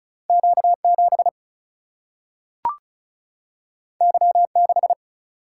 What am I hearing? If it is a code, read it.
Morse code, heard as a keyed tone, about 36 wpm, 700 Hz. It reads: Q 7 Y 6
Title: Single-Letter Single-Number 35wpm